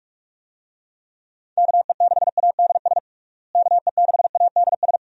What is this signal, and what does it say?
Morse code, heard as a keyed tone, about 46 wpm, 700 Hz. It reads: KE6ADS KE6ADS
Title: USA Call Signs - 6 Characters - Repeated 2x - 45wpm